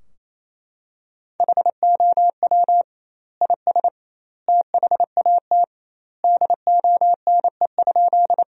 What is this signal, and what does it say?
Morse code, heard as a keyed tone, about 28 wpm, 700 Hz. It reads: HOW IS THAT DONE?
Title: 2x Word-Spacing - Sentences from Top 500 Words - 28wpm